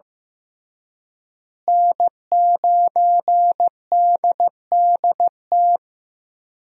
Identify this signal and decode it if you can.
Morse code, heard as a keyed tone, about 15 wpm, 700 Hz. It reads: N9DDT
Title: USA Call Signs - 5 Characters - Repeated 3x - 15wpm